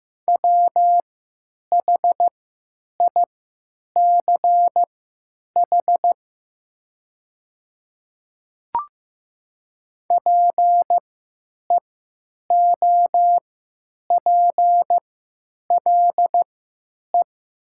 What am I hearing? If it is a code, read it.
Morse code, heard as a keyed tone, about 15 wpm, spaced as for 9 wpm, 700 Hz. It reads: WHICH PEOPLE
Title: 3x Character Spacing - Top 200 Words - 15wpm